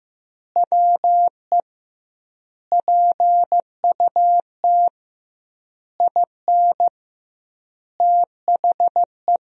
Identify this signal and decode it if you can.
Morse code, heard as a keyed tone, about 15 wpm, 700 Hz. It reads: WE PUT IN THE
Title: 2x Word-Spacing - Sentences from Top 200 Words - 15wpm